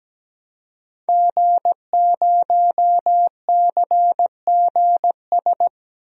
Morse code, heard as a keyed tone, about 17 wpm, 700 Hz. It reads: G0CGS
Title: DXCC-100 Call Signs - 5 Characters - 17wpm